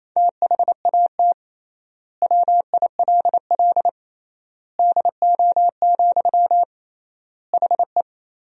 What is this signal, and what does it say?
Morse code, heard as a keyed tone, about 28 wpm, 700 Hz. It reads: THAT WILL DO, HE
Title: 3x Word-Spacing - Sentences from Top 100 Words - 28wpm